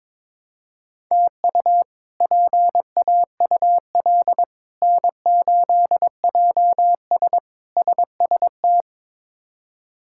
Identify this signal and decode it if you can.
Morse code, heard as a keyed tone, about 22 wpm, 700 Hz. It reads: TU PAUL N8JS SST
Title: K1USN SST QSO Practice - 22wpm